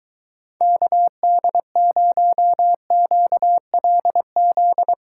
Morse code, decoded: KD0QLZ

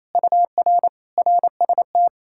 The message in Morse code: UR RST

